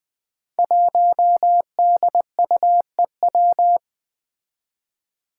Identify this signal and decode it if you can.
Morse code, heard as a keyed tone, about 20 wpm, 700 Hz. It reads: 1DUEW
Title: Five Challenging Letter-Number Combinations 20wpm